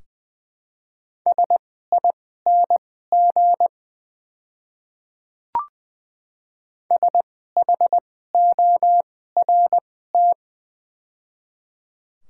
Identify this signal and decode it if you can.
Morse code, heard as a keyed tone, about 20 wpm, spaced as for 14 wpm, 700 Hz. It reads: SING SHORT
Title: 2x Character Spacing - Top 500 Words - 20wpm